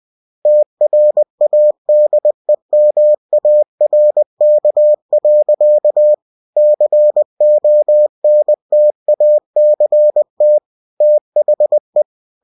Morse code, heard as a keyed tone, about 20 wpm, 600 Hz. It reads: TRADEMARK. CONTACT THE